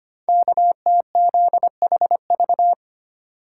Morse code, heard as a keyed tone, about 25 wpm, 700 Hz. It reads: KTZHV
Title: Code Groups - 5 Letters - 25wpm